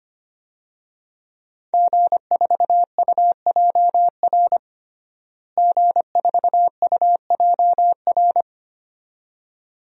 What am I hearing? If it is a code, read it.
Morse code, heard as a keyed tone, about 25 wpm, 700 Hz. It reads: G4UJR G4UJR
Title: DXCC-100 Call Signs - 5 Characters - Repeated 2x - 25wpm